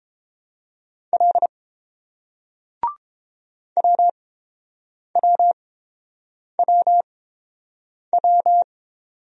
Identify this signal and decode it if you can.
Morse code, heard as a keyed tone, about 30 wpm, spaced as for 7 wpm, 690 Hz. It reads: L WWWW